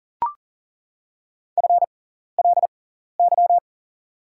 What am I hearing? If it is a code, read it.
Morse code, heard as a keyed tone, about 40 wpm, spaced as for 14 wpm, 700 Hz. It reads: FLY
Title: 6x Character Spacing - Top 500 Words - 40wpm